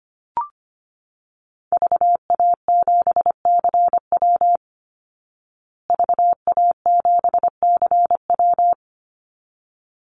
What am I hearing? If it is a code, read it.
Morse code, heard as a keyed tone, about 25 wpm, 700 Hz. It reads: VA7CW VA7CW
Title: Call Signs Repeated 3x - 25wpm